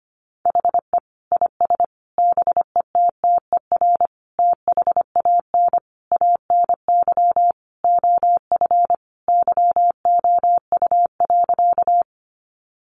Morse code, heard as a keyed tone, about 25 wpm, 700 Hz. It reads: HE IS BETTER THAN ANY OF YOU.
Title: Sentences from Top 400 Words 25wpm